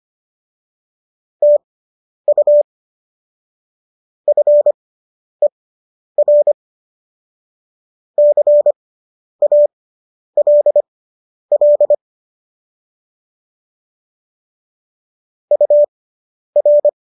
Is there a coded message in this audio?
Yes: Morse code from 1.4 to 16.9 s, heard as a keyed tone, about 25 wpm, spaced as for 10 wpm, 590 Hz. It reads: TU FER CALL UR